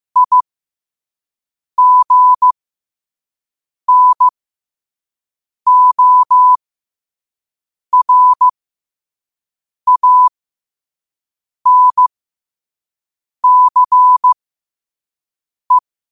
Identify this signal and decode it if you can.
Morse code, heard as a keyed tone, about 15 wpm, spaced as for 5 wpm, 1000 Hz. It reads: IGNORANCE
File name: qotdfiles / 5wpm